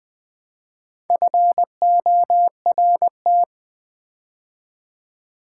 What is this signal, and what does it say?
Morse code, heard as a keyed tone, about 20 wpm, 700 Hz. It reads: FORT
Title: Build A Word - 4 Letters At A Time - 11-Letter Words - 20wpm